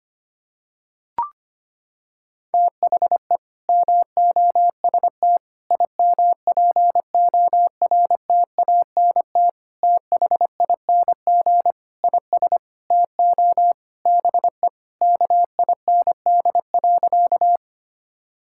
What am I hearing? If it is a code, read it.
Morse code, heard as a keyed tone, about 25 wpm, 700 Hz. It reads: THE MOST IMPORTANT THING IS TO BE KIND.
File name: practice / 25wpm / Sentences with 4-Word Phrases - 25wpm